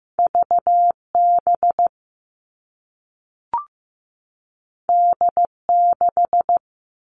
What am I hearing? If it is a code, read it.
Morse code, heard as a keyed tone, about 15 wpm, 700 Hz. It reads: VB D6
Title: All Cut Numbers - 2-Digit 15wpm